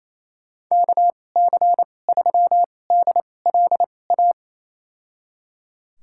Morse code, heard as a keyed tone, about 28 wpm, spaced as for 20 wpm, 700 Hz. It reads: KC3DLA
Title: USA Call Signs - 6 Characters - 2x Character Spacing - 28wpm